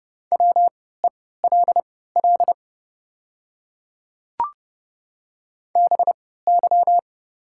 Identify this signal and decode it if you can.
Morse code, heard as a keyed tone, about 30 wpm, spaced as for 17 wpm, 700 Hz. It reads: WELL BY
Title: 3x Character Spacing - Top 200 Words - 30wpm